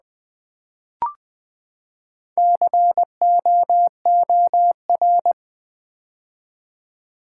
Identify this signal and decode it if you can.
Morse code, heard as a keyed tone, about 20 wpm, 700 Hz. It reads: COOR
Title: Build A Word - 4 Letters At A Time - 11-Letter Words - 20wpm